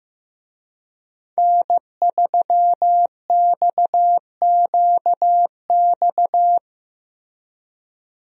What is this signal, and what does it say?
Morse code, heard as a keyed tone, about 15 wpm, 700 Hz. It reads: N3XQX
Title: USA Call Signs - 5 Characters - Repeated 2x - 15wpm